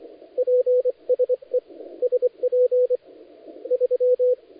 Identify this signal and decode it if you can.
Morse code, heard as a keyed tone, about 25 wpm, spaced as for 17 wpm, 500 Hz, with radio noise. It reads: PSE SP3
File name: dlazaawansowanych / nagraniazpasma / titawaNETcontrol